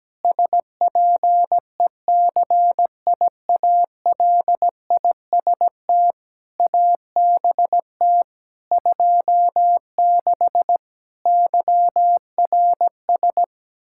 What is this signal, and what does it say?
Morse code, heard as a keyed tone, about 17 wpm, 700 Hz. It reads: SPECIALIST ABT 26 YRS
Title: Practice QSO - Exchange 3 - More Info - 17wpm